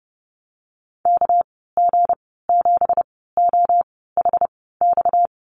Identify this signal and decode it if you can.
Morse code, heard as a keyed tone, about 30 wpm, spaced as for 17 wpm, 700 Hz. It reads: KG7OHX
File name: Lesson 37g - Letter x - Call Signs 30wpm